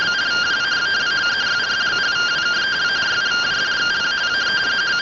Two-tone teleprinter signal, about 170 Hz shift.